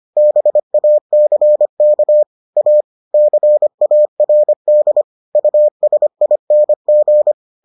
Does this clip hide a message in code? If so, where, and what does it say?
Yes: Morse code from 0.2 to 7.3 s, heard as a keyed tone, about 25 wpm, 600 Hz. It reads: BACK A CARD USING